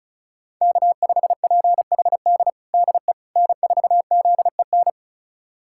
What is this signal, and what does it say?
Morse code, heard as a keyed tone, about 34 wpm, 700 Hz. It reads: K5PHD DE N4ZEN
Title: USA Callsign DE Callsign - 5 Characters - 35wpm